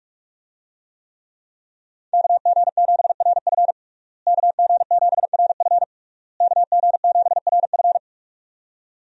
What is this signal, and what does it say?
Morse code, heard as a keyed tone, about 41 wpm, 700 Hz. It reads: KG7RF KG7RF KG7RF
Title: USA Call Signs - 5 Characters - Repeated 3x - 45wpm